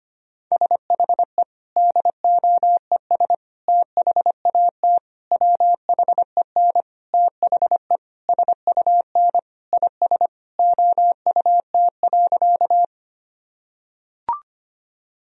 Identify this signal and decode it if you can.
Morse code, heard as a keyed tone, about 25 wpm, 700 Hz. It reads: SHE DOES THAT WHEN THE SUN IS OUT.
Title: Sentences from Top 1000 Words - Review Part 1 - 25wpm